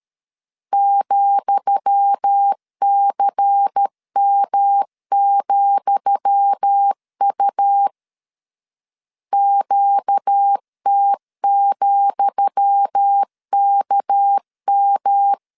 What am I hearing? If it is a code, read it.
Morse code, heard as a keyed tone, about 13 wpm, 800 Hz. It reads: ,CM,U QT,KM
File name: LEZIONI / 10 SIGNS / 10D COMA 12wpm 2544